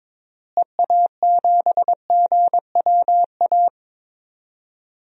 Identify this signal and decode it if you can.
Morse code, heard as a keyed tone, about 21 wpm, 700 Hz. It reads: EA7GWA
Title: DXCC-100 Call Signs - 6 Characters - Repeated 3x - 22wpm